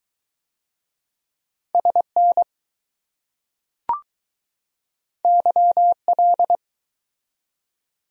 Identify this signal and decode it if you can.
Morse code, heard as a keyed tone, about 20 wpm, 700 Hz. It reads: SN YL